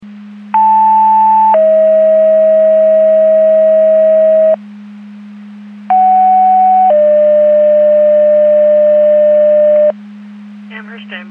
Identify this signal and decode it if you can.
Two-tone page, heard as a one-second tone, then a three-second tone, about 880 then 640 Hz, about 750 then 600 Hz. It is heard from 0.5 to 9.9 s.